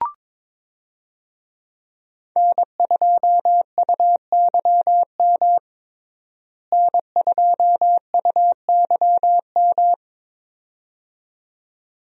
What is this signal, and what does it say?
Morse code, heard as a keyed tone, about 22 wpm, 700 Hz. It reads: N2UYM N2UYM
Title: USA Call Signs - 5 Characters - Repeated 2x - 22wpm